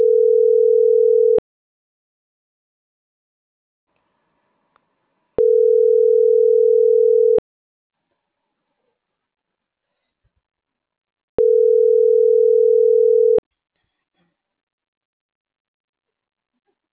Ringback tone (North America), 440 + 480 Hz.